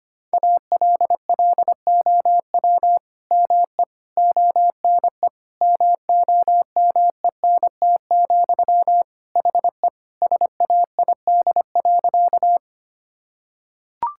Morse code, heard as a keyed tone, about 25 wpm, 700 Hz. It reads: ALLOW ME ONE MOMENT, HE SAID.
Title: Sentences from Top 1000 Words - Review Part 5 - 25wpm